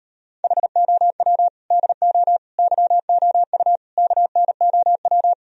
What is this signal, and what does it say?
Morse code, heard as a keyed tone, about 38 wpm, 700 Hz. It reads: HOW DO YOU KNOW